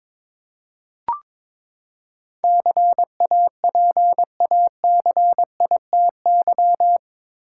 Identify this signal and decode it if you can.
Morse code, heard as a keyed tone, about 22 wpm, 700 Hz. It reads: CAPACITY
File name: practice / 22wpm / Eight Letter Words - 22wpm